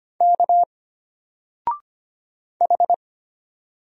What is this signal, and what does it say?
Morse code, heard as a keyed tone, about 25 wpm, 700 Hz. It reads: K H
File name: Single-Letter-Number-25wpm